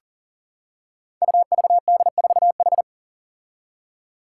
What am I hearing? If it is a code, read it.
Morse code, heard as a keyed tone, about 40 wpm, 700 Hz. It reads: UVD4H
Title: Five Challenging Letter-Number Combinations 40wpm